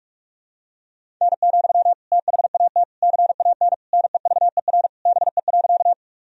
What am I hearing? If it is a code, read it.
Morse code, heard as a keyed tone, about 46 wpm, 700 Hz. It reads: N, THAT CAN NEVER BE.
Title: Sentences from Top 300 Words and Abbreviations 45wpm